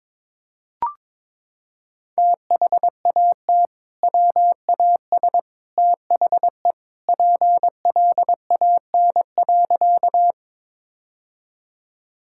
Morse code, heard as a keyed tone, about 22 wpm, 700 Hz. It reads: THAT WAS THE PLAN.
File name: practice / 22wpm / Sentences with 2-Word Phrases - 22wpm